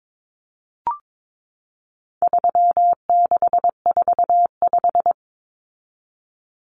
Morse code, heard as a keyed tone, about 22 wpm, 700 Hz. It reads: 3645
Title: Common Cut Numbers - 4-Digit 22wpm